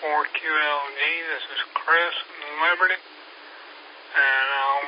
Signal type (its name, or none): Emergency Alert System (SAME) tones